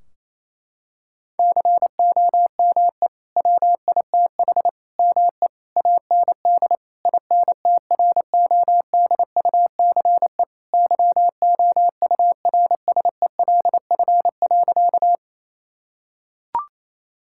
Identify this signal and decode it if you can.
Morse code, heard as a keyed tone, about 28 wpm, 700 Hz. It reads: COME WITH ME AND INTRODUCE YOURSELF.
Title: Sentences from Top 1600 Words 28wpm